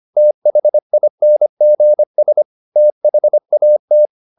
Morse code, heard as a keyed tone, about 25 wpm, 600 Hz. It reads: THINGS THAT